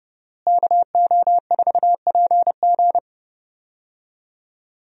Morse code, heard as a keyed tone, about 30 wpm, 700 Hz. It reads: KO4PG
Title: USA Call Signs - 5 Characters - Repeated 3x - 30wpm